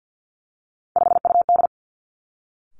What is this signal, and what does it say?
Morse code, heard as a keyed tone, about 50 wpm, 700 Hz. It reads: 5UD
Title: Three Challenging Letter-Number Combinations 50wpm